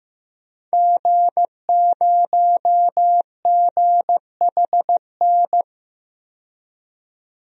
Morse code, heard as a keyed tone, about 15 wpm, 700 Hz. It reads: G0GHN
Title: DXCC-100 Call Signs - 5 Characters - 15wpm